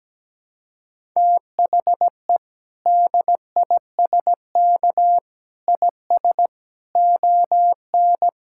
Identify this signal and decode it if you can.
Morse code, heard as a keyed tone, about 17 wpm, 700 Hz. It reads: THE DISK IS ON
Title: Sentences from Top 1000 Words - Review Part 6 - 17wpm